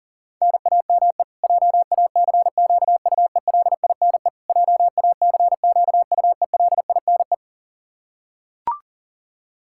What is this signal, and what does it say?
Morse code, heard as a keyed tone, about 40 wpm, 700 Hz. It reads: NAME JACQUELINE JACQUELINE
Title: K1USN SST QSO Practice 40wpm